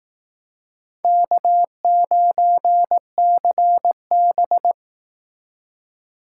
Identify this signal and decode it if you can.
Morse code, heard as a keyed tone, about 19 wpm, 700 Hz. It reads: K9CB
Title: Speed Racing - USA Call Signs 15wpm